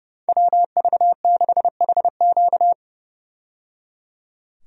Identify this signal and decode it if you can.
Morse code, heard as a keyed tone, about 30 wpm, 700 Hz. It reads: WV6HQ
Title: USA Call Signs - 5 Characters - 30wpm